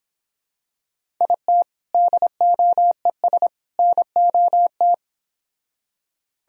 Morse code, heard as a keyed tone, about 26 wpm, 700 Hz. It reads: IT DOES NOT